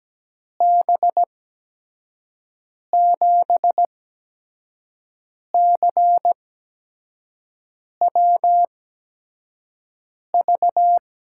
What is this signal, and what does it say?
Morse code, heard as a keyed tone, about 17 wpm, spaced as for 5 wpm, 700 Hz. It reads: B7CWV
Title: USA Call Signs - 6 Characters - 8x Character Spacing - 17wpm